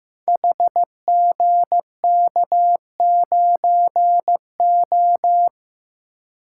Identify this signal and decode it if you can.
Morse code, heard as a keyed tone, about 15 wpm, 700 Hz. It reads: HGK9O
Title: Code Groups - 5 Letter:Numbers - 15wpm